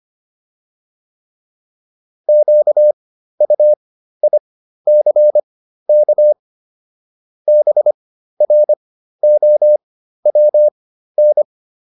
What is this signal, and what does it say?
Morse code, heard as a keyed tone, about 25 wpm, spaced as for 13 wpm, 600 Hz. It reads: QUICK BROWN
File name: PR404 13